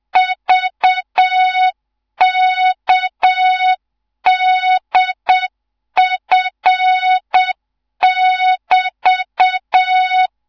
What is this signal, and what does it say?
Morse code, heard as a keyed tone, about 7 wpm, 3700 Hz, with noise. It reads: VKDF<BT>
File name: Morze-HA2MN / hang / 14 280 HA2MN